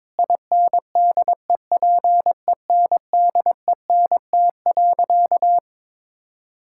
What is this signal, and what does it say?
Morse code, heard as a keyed tone, about 22 wpm, 700 Hz. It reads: INDEPENDENT.
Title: Sentences from Top 1000 Words - Review Part 6 - 22wpm